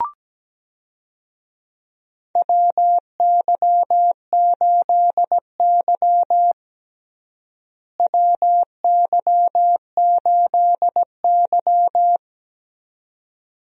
Morse code, heard as a keyed tone, about 17 wpm, 700 Hz. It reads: WY8Y WY8Y
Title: USA Call Signs - 4 Characters - Repeated 2x - 17wpm